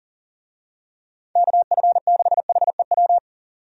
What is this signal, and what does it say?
Morse code, heard as a keyed tone, about 40 wpm, 700 Hz. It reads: KF6HEW
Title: USA Call Signs - 6 Characters - 40wpm